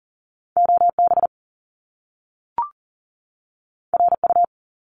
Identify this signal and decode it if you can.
Morse code, heard as a keyed tone, about 40 wpm, 700 Hz. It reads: OB RU
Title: Top Two Letter Combinations in Words 40wpm